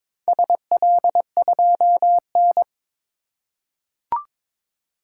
Morse code, heard as a keyed tone, about 22 wpm, 700 Hz. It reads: SL2N
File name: practice / 22wpm / Four Letter-Number Combinations - 22wpm